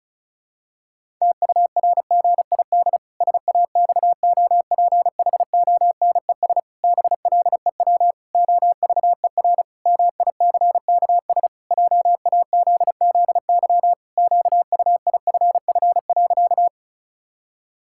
Morse code, heard as a keyed tone, about 34 wpm, 700 Hz. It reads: TURGID SAXOPHONES BLEW OVER MICKS JAZZY QUIFF.
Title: Pangram Challenge v2 - 35wpm